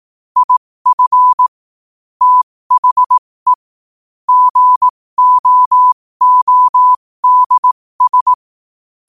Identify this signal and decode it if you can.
Morse code, heard as a keyed tone, about 18 wpm, spaced as for 15 wpm, 1000 Hz. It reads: IF THE GOODS
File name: qotdfiles / 15wpm